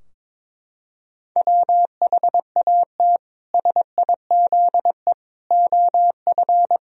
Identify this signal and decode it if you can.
Morse code, heard as a keyed tone, about 22 wpm, 700 Hz. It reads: WHAT SIZE OF
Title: Sentences from Top 1000 Words - Review Part 2 - 22wpm